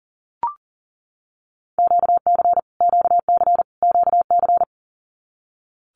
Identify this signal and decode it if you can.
Morse code, heard as a keyed tone, about 40 wpm, 700 Hz. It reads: QC QC QC
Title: Canadian Provinces and Territories - Repeated 3x - 40wpm